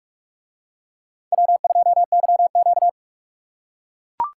Morse code, heard as a keyed tone, about 42 wpm, 700 Hz. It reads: W2YQ